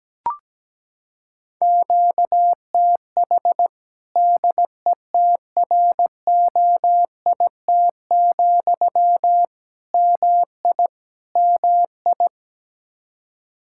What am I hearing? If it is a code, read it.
Morse code, heard as a keyed tone, about 17 wpm, 700 Hz. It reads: QTH DETROIT, MI MI
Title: Practice QSO - Exchange 1 - Beginning - 17wpm